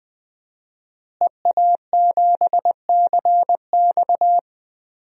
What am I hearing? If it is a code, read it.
Morse code, heard as a keyed tone, about 19 wpm, 700 Hz. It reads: EA7CX
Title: DXCC-100 Call Signs - 5 Characters - Repeated 2x - 20wpm